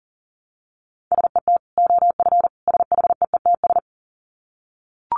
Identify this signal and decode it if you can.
Morse code, heard as a keyed tone, about 40 wpm, 700 Hz. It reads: SET OF SHEETS
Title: Dit Challenge 40wpm